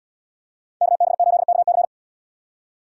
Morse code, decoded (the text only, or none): KB8KC